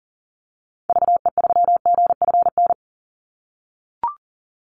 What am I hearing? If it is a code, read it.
Morse code, heard as a keyed tone, about 37 wpm, 700 Hz. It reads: VE3GFN
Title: Call Signs - 40wpm